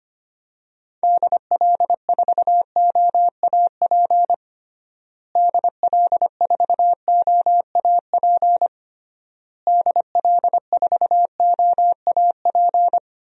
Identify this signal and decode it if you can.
Morse code, heard as a keyed tone, about 25 wpm, 700 Hz. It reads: DL4OAP DL4OAP DL4OAP
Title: USA Call Signs - 6 Characters - Repeated 3x - 25wpm